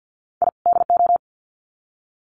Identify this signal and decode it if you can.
Morse code, heard as a keyed tone, about 50 wpm, 700 Hz. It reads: I DO
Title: Sets of 2 Words 50wpm